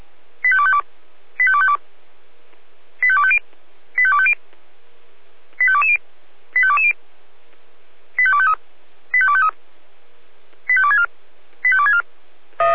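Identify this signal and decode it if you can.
Five-tone selective call (ZVEI), heard as quick runs of five tones, tones of about 70 ms. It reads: 85252 85252 85250 85250 85220 85220 85253 85253 85264 85264
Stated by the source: ZVEI five-tone selective call on 496, received by SDR.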